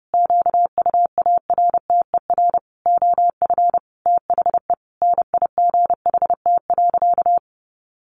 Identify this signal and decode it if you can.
Morse code, heard as a keyed tone, about 30 wpm, 700 Hz. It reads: QUARTER OF THE NIGHT.
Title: Sentences from Top 1300 Words 30wpm